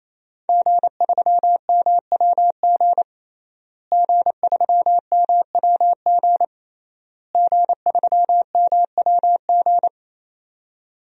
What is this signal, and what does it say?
Morse code, heard as a keyed tone, about 27 wpm, 700 Hz. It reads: G3MWG G3MWG G3MWG
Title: DXCC-100 Call Signs - 5 Characters - Repeated 3x - 28wpm